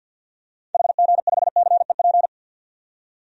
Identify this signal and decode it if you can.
Morse code, heard as a keyed tone, about 51 wpm, 700 Hz. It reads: HG5CEP
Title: DXCC-100 Call Signs - 6 Characters - 50wpm